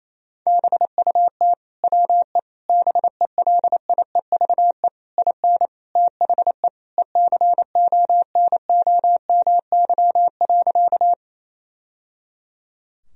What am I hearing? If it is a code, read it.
Morse code, heard as a keyed tone, about 28 wpm, 700 Hz. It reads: BUT WE BELIEVE IN THE ECONOMY.
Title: Sentences from Top 1000 Words - Review Part 3 - 28wpm